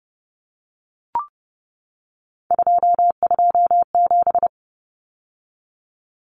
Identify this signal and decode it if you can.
Morse code, heard as a keyed tone, about 30 wpm, 700 Hz. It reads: 227
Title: Numbers - 3-Digit 30wpm